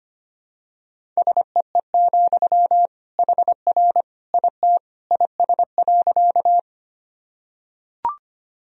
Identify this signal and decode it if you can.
Morse code, heard as a keyed tone, about 25 wpm, 700 Hz. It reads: SEE, HR IT IS.
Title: Sentences from Top 200 Words and Abbreviations 25wpm